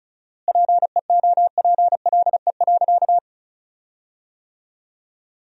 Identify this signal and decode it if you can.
Morse code, heard as a keyed tone, about 34 wpm, 700 Hz. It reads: PEOPLE.
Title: 4x Word-Spacing - Sentences from Top 300 Words - 35wpm